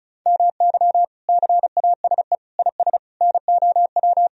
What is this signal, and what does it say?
Morse code, heard as a keyed tone, about 36 wpm, 700 Hz. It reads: MY CASE IS NOW
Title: Sentences from Top 1000 Words - Review Part 2 - 35wpm